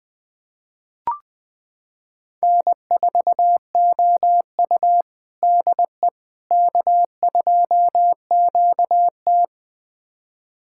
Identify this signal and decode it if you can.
Morse code, heard as a keyed tone, about 20 wpm, 700 Hz. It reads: N4OU DE K2QT
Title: USA Callsign DE Callsign - 4 Characters - 20wpm